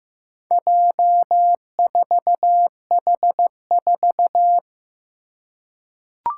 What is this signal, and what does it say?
Morse code, heard as a keyed tone, about 15 wpm, 700 Hz. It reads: J4H4
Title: Four Challenging Letter-Number Combinations 15wpm